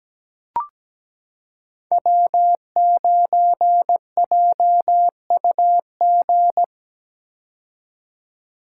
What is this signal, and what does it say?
Morse code, heard as a keyed tone, about 17 wpm, 700 Hz. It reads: W9JUG
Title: K1USN SST QSO Practice - 17wpm